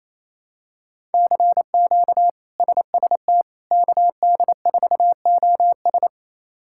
Morse code, heard as a keyed tone, about 27 wpm, 700 Hz. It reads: CQ SST KD4OS
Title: K1USN SST QSO Practice - 28wpm